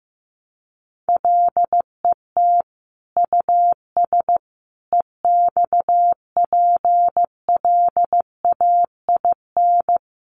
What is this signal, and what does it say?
Morse code, heard as a keyed tone, about 15 wpm, 700 Hz. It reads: LET US EXPLAIN
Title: Sentences from Top 1100 Words 15wpm